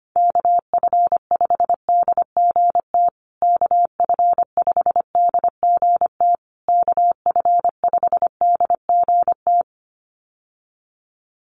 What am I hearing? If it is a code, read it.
Morse code, heard as a keyed tone, about 25 wpm, 700 Hz. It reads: KF5DGT KF5DGT KF5DGT
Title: USA Call Signs Repeated 3x 25wpm